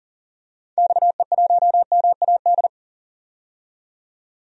Morse code, heard as a keyed tone, about 40 wpm, 700 Hz. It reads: XE1MAD
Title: DXCC-100 Call Signs - 6 Characters - 40wpm